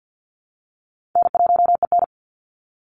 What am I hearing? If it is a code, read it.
Morse code, heard as a keyed tone, about 50 wpm, 700 Hz. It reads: N1EN
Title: Call Signs Repeated 2x - 50wpm